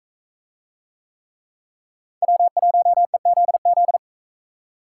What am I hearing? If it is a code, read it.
Morse code, heard as a keyed tone, about 37 wpm, 700 Hz. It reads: W1EZZ